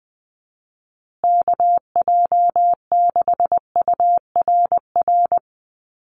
Morse code, heard as a keyed tone, about 20 wpm, 700 Hz. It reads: KJ6URR
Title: USA Call Signs Repeated 2x 20wpm